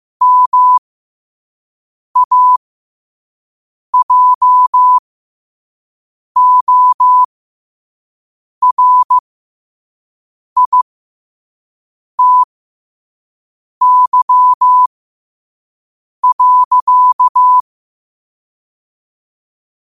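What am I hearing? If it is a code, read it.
Morse code, heard as a keyed tone, about 15 wpm, spaced as for 5 wpm, 1000 Hz. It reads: MAJORITY.
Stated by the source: Morse code sent at 5 wpm.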